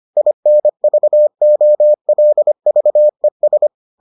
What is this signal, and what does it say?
Morse code, heard as a keyed tone, about 25 wpm, 600 Hz. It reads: INVOLVES